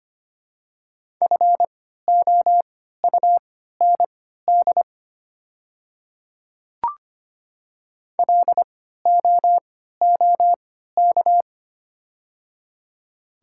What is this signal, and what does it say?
Morse code, heard as a keyed tone, about 25 wpm, spaced as for 14 wpm, 700 Hz. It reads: FOUND LOOK
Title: 3x Character Spacing - Top 300 Words - 25wpm